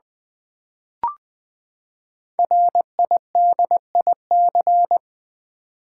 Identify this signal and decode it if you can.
Morse code, heard as a keyed tone, about 20 wpm, 700 Hz. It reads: RIDIC
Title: Build A Word - 5 Letters At A Time - 10-Letter Words - 20wpm